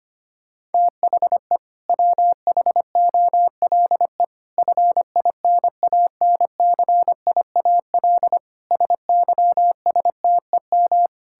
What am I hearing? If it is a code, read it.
Morse code, heard as a keyed tone, about 25 wpm, 700 Hz. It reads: THE WHOLE FINANCIAL SYSTEM
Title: Sentences from Top 1000 Words - Review Part 5 - 25wpm